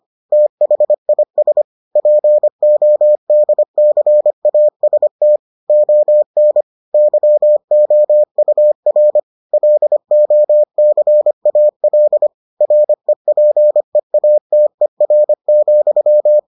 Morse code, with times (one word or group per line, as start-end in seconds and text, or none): 0.3-1.6 THIS
2.0-5.4 PODCAST
5.7-6.6 ON
6.9-9.2 YOUR
9.5-12.3 LOCAL
12.6-16.4 REPEATER,